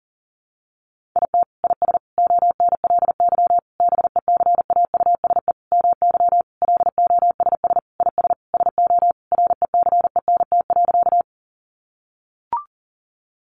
Morse code, heard as a keyed tone, about 40 wpm, 700 Hz. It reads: IT IS ONLY BECAUSE MY LOSS IS SO RECENT.
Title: Sentences from Top 1400 Words 40wpm